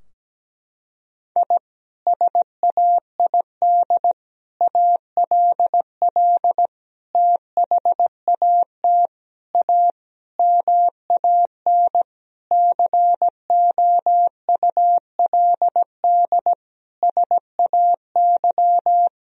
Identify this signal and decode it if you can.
Morse code, heard as a keyed tone, about 17 wpm, 700 Hz. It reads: I SAID ALL THAT A MAN COULD SAY
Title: Sentences from Top 1000 Words - Review Part 2 - 17wpm